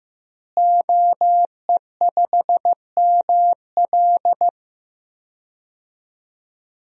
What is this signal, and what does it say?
Morse code, heard as a keyed tone, about 15 wpm, 700 Hz. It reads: OE5ML
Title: DXCC-100 Call Signs - 5 Characters - Repeated 3x - 15wpm